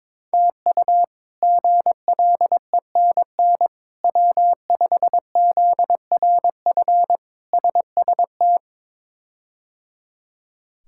Morse code, heard as a keyed tone, about 21 wpm, 700 Hz. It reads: TU GLENN W5ZRF SST
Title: K1USN SST QSO Practice - 22wpm